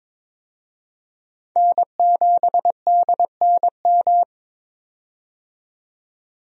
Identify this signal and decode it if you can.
Morse code, heard as a keyed tone, about 22 wpm, 700 Hz. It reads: N7DNM
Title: USA Call Signs - 5 Characters - 22wpm